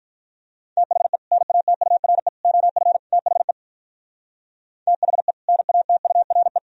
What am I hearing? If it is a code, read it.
Morse code, heard as a keyed tone, about 45 wpm, 700 Hz. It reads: THE NATURE OF THE THE NATURE